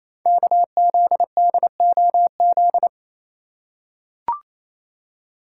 Morse code, heard as a keyed tone, about 28 wpm, 700 Hz. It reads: KZDOZ
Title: Code Groups - 5 Letter:Numbers - 28wpm